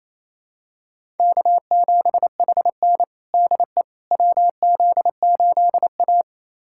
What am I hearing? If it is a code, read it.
Morse code, heard as a keyed tone, about 28 wpm, 700 Hz. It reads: K7HN DE WZ8A
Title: USA Callsign DE Callsign - 4 Characters - 28wpm